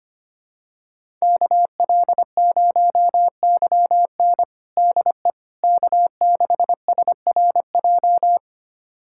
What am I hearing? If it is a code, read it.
Morse code, heard as a keyed tone, about 25 wpm, 700 Hz. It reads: KL0YN DE K6SRJ
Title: USA Callsign DE Callsign - 5 Characters - 25wpm